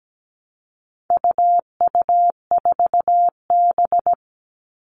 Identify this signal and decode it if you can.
Morse code, heard as a keyed tone, about 17 wpm, 700 Hz. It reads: UU4B
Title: All Cut Numbers - 4-Digit 17wpm